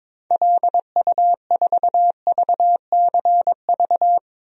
Morse code, heard as a keyed tone, about 23 wpm, 700 Hz. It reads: LU4VCV
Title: DXCC-100 Call Signs - 6 Characters - 22wpm